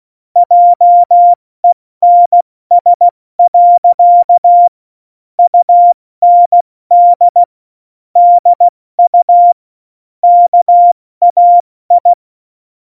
Morse code, heard as a keyed tone, about 16 wpm, 700 Hz. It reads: JENS. UND DU KAI
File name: Morsekurs / Witze / Witz 30. April 2019, 70 F80